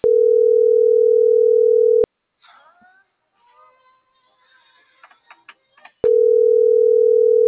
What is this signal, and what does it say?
Ringback tone (North America), 440 + 480 Hz.